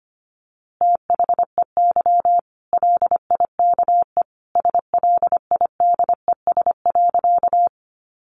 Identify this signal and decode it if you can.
Morse code, heard as a keyed tone, about 25 wpm, 700 Hz. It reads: THEY LIKE SLIDES.
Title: Easy Phrases 25wpm